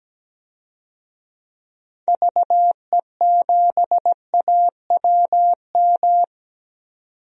Morse code, heard as a keyed tone, about 17 wpm, 700 Hz. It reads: VE7AWM
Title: DXCC-100 Call Signs - 6 Characters - 17wpm